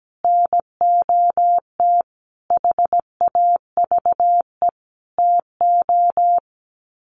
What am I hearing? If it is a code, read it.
Morse code, heard as a keyed tone, about 17 wpm, 700 Hz. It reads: NOT HAVE TO
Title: Sets of 5 Words Encore 17wpm